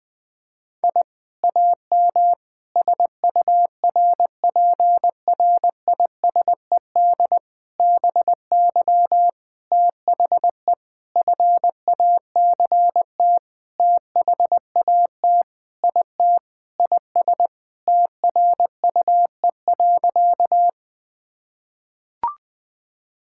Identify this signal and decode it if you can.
Morse code, heard as a keyed tone, about 20 wpm, 700 Hz. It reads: I AM SURPRISED BY THE FACT THAT IT IS TRUE.
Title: Sentences with 4-Word Phrases - 20wpm